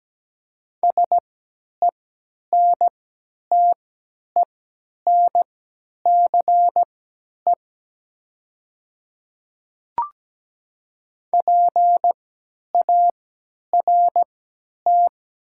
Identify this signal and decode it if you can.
Morse code, heard as a keyed tone, about 17 wpm, spaced as for 10 wpm, 700 Hz. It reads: SENTENCE PART
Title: 3x Character Spacing - Top 200 Words - 17wpm